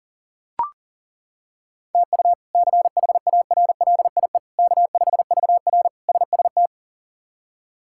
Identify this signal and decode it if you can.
Morse code, heard as a keyed tone, about 37 wpm, 700 Hz. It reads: TU CHARLIE K5VR SST
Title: K1USN SST QSO Practice - 40wpm